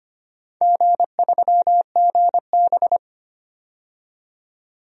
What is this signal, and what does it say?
Morse code, heard as a keyed tone, about 25 wpm, 700 Hz. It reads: G3GB
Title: DXCC-100 Call Signs - 4 Characters - 25wpm